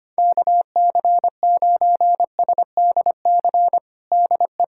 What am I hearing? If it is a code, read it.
Morse code, heard as a keyed tone, about 25 wpm, 700 Hz. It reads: KC9SDC DE
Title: USA Callsign DE Callsign - 6 Characters - 25wpm